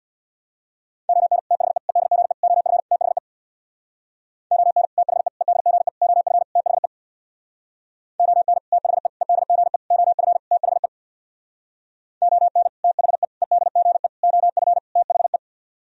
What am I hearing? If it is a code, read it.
Morse code, heard as a keyed tone, about 64 wpm, 700 Hz. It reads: ON THE EDGE OF THE ON THE EDGE OF THE ON THE EDGE OF THE ON THE EDGE OF THE